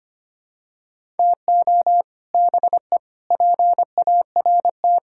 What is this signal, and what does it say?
Morse code, heard as a keyed tone, about 25 wpm, 700 Hz. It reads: TO BE PART
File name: practice / 25wpm / Sentences with 4-Word Phrases - 25wpm